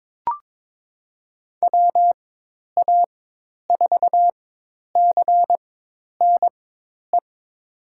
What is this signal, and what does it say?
Morse code, heard as a keyed tone, about 22 wpm, spaced as for 10 wpm, 700 Hz. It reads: WA4CNE
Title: USA Call Signs - 4x Character Spacing - 22wpm